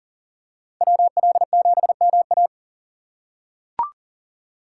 Morse code, heard as a keyed tone, about 40 wpm, 700 Hz. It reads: WL7MA
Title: USA Call Signs - 5 Characters - 3x Character Spacing - 40wpm